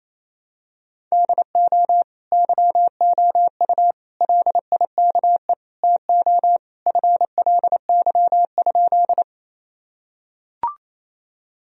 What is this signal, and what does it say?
Morse code, heard as a keyed tone, about 28 wpm, 700 Hz. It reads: DO YOU LIKE TO FLY?
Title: Sentences from Top 1000 Words - Review Part 2 - 28wpm